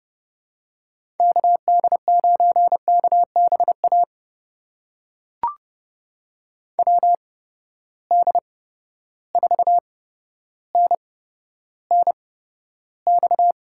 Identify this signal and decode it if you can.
Morse code, heard as a keyed tone, about 30 wpm, spaced as for 13 wpm, 700 Hz. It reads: KD9KBA WD4NNX
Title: USA Call Signs - 6 Characters - 8x Character Spacing - 30wpm